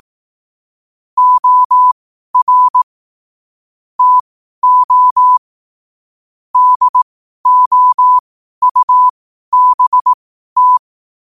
Morse code, heard as a keyed tone, about 18 wpm, spaced as for 12 wpm, 1000 Hz. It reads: OR TO DOUBT
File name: qotdfiles / 12wpm